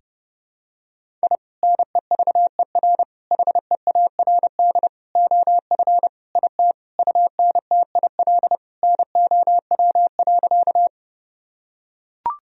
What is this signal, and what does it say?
Morse code, heard as a keyed tone, about 30 wpm, 700 Hz. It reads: I NEVER HEARD OF IT UNTIL NOW.
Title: Sentences from Top 1000 Words - Review Part 1 - 30wpm